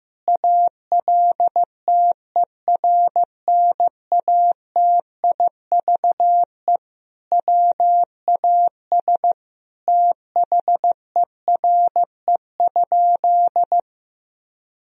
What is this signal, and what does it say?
Morse code, heard as a keyed tone, about 15 wpm, 700 Hz. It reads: ALTERNATIVE WAS THERE?
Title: Sentences from Top 1000 Words - Review Part 4 - 15wpm